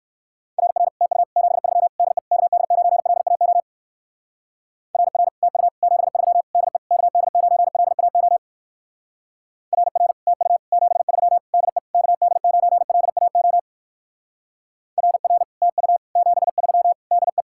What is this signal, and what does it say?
Morse code, heard as a keyed tone, about 60 wpm, 700 Hz. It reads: RR TU 73 DE KD9LAO RR TU 73 DE KD9LAO RR TU 73 DE KD9LAO RR TU 73 DE